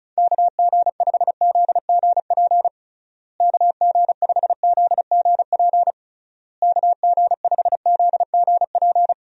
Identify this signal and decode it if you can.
Morse code, heard as a keyed tone, about 36 wpm, 700 Hz. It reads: KG5ZGP KG5ZGP KG5ZGP
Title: USA Call Signs - 6 Characters - Repeated 3x - 35wpm